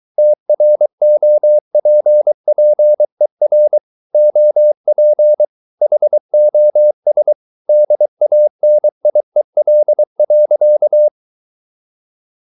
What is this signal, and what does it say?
Morse code, heard as a keyed tone, about 23 wpm, 600 Hz. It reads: TROPPER OP HOS DANIEL.
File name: Ferie QRS 23wpm